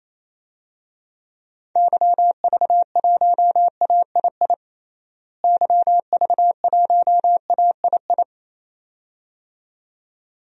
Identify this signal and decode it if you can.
Morse code, heard as a keyed tone, about 28 wpm, 700 Hz. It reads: YV1AII YV1AII
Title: DXCC-100 Call Signs - 6 Characters - Repeated 2x - 28wpm